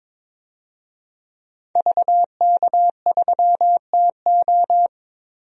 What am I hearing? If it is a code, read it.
Morse code, heard as a keyed tone, about 22 wpm, 700 Hz. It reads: VK3TO
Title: DXCC-100 Call Signs - 5 Characters - Repeated 3x - 22wpm